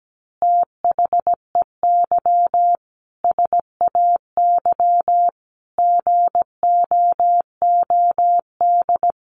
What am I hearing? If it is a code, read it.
Morse code, heard as a keyed tone, about 17 wpm, 700 Hz. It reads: THEY SAY GOOD
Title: Easy Phrases 17wpm